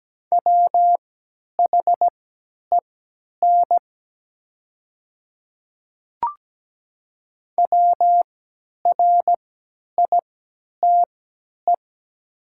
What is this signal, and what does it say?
Morse code, heard as a keyed tone, about 17 wpm, spaced as for 10 wpm, 700 Hz. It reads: WHEN WRITE
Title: 3x Character Spacing - Top 100 Words - 17wpm